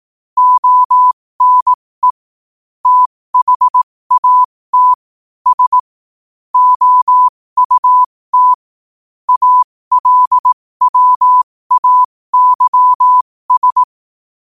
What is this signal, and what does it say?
Morse code, heard as a keyed tone, about 18 wpm, spaced as for 15 wpm, 1000 Hz. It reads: ONE THAT S OUT ALWAYS
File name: qotdfiles / 15wpm